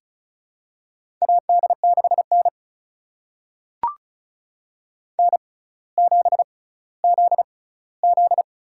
Morse code, heard as a keyed tone, about 36 wpm, spaced as for 18 wpm, 700 Hz. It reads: AD6N N7ZZ
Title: USA Call Signs - 4 Characters - 6x Character Spacing - 35wpm